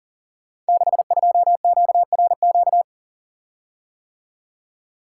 Morse code, heard as a keyed tone, about 40 wpm, 690 Hz. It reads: 62QRQ
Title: Code Groups - 5 Letter:Numbers - 40wpm